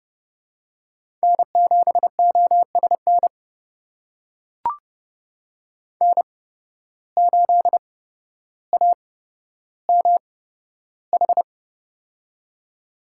Morse code, heard as a keyed tone, about 32 wpm, spaced as for 13 wpm, 700 Hz. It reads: N7OSN N8AMH
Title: USA Call Signs - 5 Characters - 8x Character Spacing - 30wpm